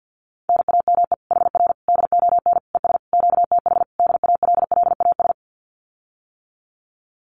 Morse code, heard as a keyed tone, about 48 wpm, 700 Hz. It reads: NAME HR DON ES QTH DALLAS
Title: Common QSO Phrases Encore 50wpm